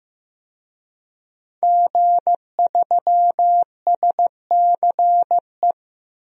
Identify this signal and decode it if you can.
Morse code, heard as a keyed tone, about 15 wpm, 700 Hz. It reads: G3SCE
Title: DXCC-100 Call Signs - 5 Characters - Repeated 2x - 15wpm